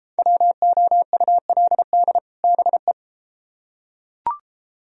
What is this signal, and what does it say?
Morse code, heard as a keyed tone, about 28 wpm, 700 Hz. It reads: WOULD BE